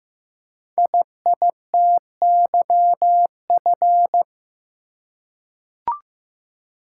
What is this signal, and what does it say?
Morse code, heard as a keyed tone, about 15 wpm, 700 Hz. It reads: IITYF
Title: Code Groups - 5 Letters - 15wpm